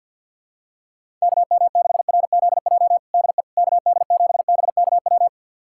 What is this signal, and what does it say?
Morse code, heard as a keyed tone, about 51 wpm, 700 Hz. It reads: KM6RZJ DE KD7BKW
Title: USA Callsign DE Callsign - 6 Characters - 50wpm